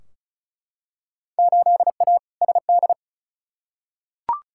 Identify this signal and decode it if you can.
Morse code, heard as a keyed tone, about 36 wpm, 700 Hz. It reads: 8A SD
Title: Field Day Exchanges Encore - 35wpm